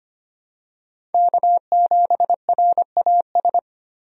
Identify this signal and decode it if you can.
Morse code, heard as a keyed tone, about 25 wpm, 690 Hz. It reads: K7RAS
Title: USA Call Signs - 5 Characters - 25wpm